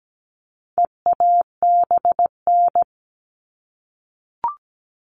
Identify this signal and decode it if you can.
Morse code, heard as a keyed tone, about 17 wpm, 700 Hz. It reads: EABN
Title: All Cut Numbers - 4-Digit 17wpm